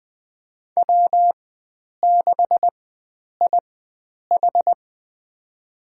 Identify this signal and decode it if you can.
Morse code, heard as a keyed tone, about 21 wpm, spaced as for 9 wpm, 700 Hz. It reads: W6IH
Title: USA Call Signs - 4 Characters - 4x Character Spacing - 20wpm